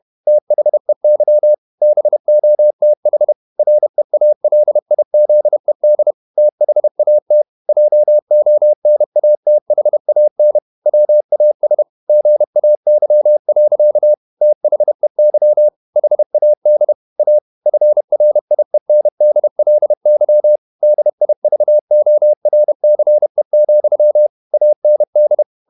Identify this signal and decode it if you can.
Morse code, heard as a keyed tone, about 31 wpm, 600 Hz. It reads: THEY BOTH REALIZED THAT JONATHAN WAS GAY. THEY HAD A FRIENDLY DIVORCE, AND
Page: Never 31wpm | CW med Gnister